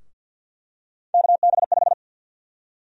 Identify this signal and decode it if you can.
Morse code, heard as a keyed tone, about 51 wpm, 700 Hz. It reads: KB5
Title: USA Call Sign Prefixes - 50wpm